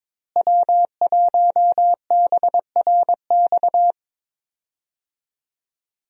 Morse code, heard as a keyed tone, about 22 wpm, 690 Hz. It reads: W1BRX
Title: USA Call Signs - 5 Characters - Repeated 3x - 22wpm